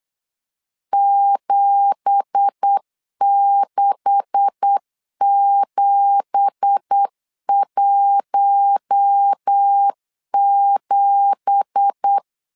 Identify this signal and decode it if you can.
Morse code, heard as a keyed tone, about 8 wpm, 800 Hz. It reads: 76717
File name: LEZIONI / 9 2867 8wpm 3440